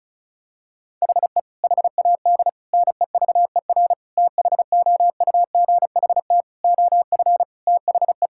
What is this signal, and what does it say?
Morse code, heard as a keyed tone, about 34 wpm, 700 Hz. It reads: HE HAD NEVER THOUGHT OF THE
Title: Sentences from Top 1600 Words 35wpm